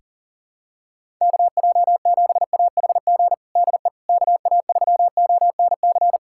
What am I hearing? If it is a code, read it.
Morse code, heard as a keyed tone, about 40 wpm, 700 Hz. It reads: KJ7AHG DE KA3ONC
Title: USA Callsign DE Callsign 40wpm